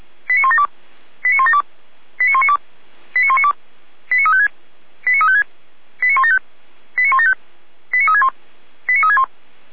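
Five-tone selective call (ZVEI), heard as quick runs of five tones, tones of about 70 ms. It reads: 89172 89172 89192 89192 89356 89356 89175 89175 89361 89361
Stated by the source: ZVEI five-tone selective call received on an SDR.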